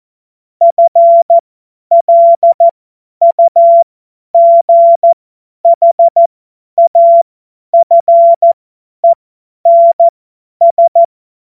Morse code, heard as a keyed tone, about 14 wpm, spaced as for 10 wpm, 680 Hz. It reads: FLUGHAFENS
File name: Morsekurs / Witze / Witz 28. März 2023 50 F70